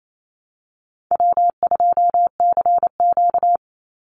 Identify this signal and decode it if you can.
Morse code, heard as a keyed tone, about 28 wpm, 700 Hz. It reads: W2CQ
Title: Easy Call Signs Repeated 2x - 28wpm